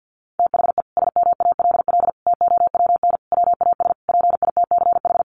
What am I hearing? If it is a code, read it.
Morse code, heard as a keyed tone, about 50 wpm, 700 Hz. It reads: THE SMALL TOWN WAS PITCH
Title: Sentences from Top 1300 Words 50wpm